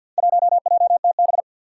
Morse code, heard as a keyed tone, about 46 wpm, 700 Hz. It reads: 1JTB
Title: Four Challenging Letter-Number Combinations 50wpm